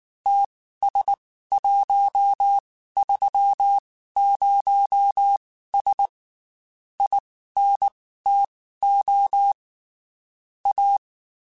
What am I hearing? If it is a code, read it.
Morse code, heard as a keyed tone, about 19 wpm, spaced as for 14 wpm, 800 Hz. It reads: TS130S INTO A
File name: QSOs / QSOs-mp3s / WD0BGP 1@18